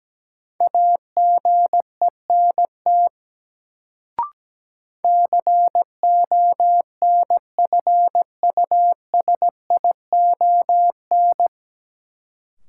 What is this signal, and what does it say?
Morse code, heard as a keyed tone, about 17 wpm, 700 Hz. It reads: AGENT CONFUSION
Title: Sentences from Top 1600 Words - New Word Drill Alt - 17wpm